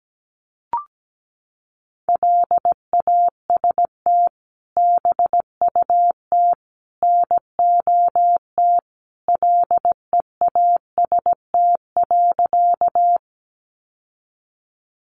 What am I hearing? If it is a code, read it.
Morse code, heard as a keyed tone, about 17 wpm, 700 Hz. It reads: LAST BUT NOT LEAST.
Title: Sentences from Top 700 Words 17wpm